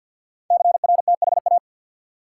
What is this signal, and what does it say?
Morse code, heard as a keyed tone, about 52 wpm, 700 Hz. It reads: CRTHA
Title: Code Groups - 5 Letters - 50wpm